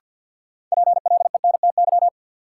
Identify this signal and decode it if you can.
Morse code, heard as a keyed tone, about 47 wpm, 700 Hz. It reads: PLENTY
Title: Sentences from Top 1300 Words - New Word Drill - 50wpm